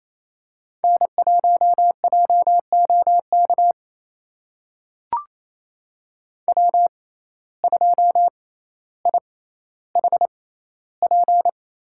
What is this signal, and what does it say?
Morse code, heard as a keyed tone, about 28 wpm, spaced as for 14 wpm, 700 Hz. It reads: N1JOK W2IHP
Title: USA Call Signs - 5 Characters - 6x Character Spacing - 28wpm